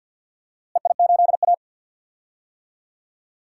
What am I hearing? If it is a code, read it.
Morse code, heard as a keyed tone, about 48 wpm, 700 Hz. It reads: EI8A
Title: DXCC-100 Call Signs - 4 Characters - Repeated 3x - 50wpm